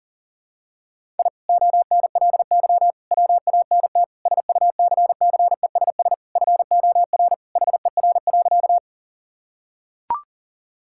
Morse code, heard as a keyed tone, about 40 wpm, 700 Hz. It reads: I ONLY WANT SUCCESS FOR HER.
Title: Sentences from Top 1000 Words - Review Part 6 - 40wpm